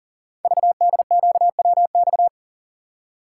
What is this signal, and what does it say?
Morse code, heard as a keyed tone, about 38 wpm, 700 Hz. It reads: VDQWX
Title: Code Groups - 5 Letter:Numbers - 40wpm